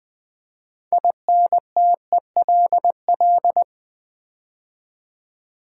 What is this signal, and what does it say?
Morse code, heard as a keyed tone, about 20 wpm, 700 Hz. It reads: INTELL
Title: Build A Word - 6 Letters At A Time - 12-Letter Words - 20wpm